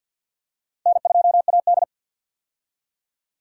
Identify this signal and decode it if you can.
Morse code, heard as a keyed tone, about 46 wpm, 700 Hz. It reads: N2AD
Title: USA Call Signs - 4 Characters - Repeated 3x - 50wpm